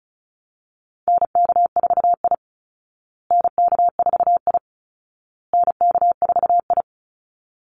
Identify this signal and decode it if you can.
Morse code, heard as a keyed tone, about 34 wpm, 700 Hz. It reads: NK4I NK4I NK4I
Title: Call Signs Repeated 2x - 35wpm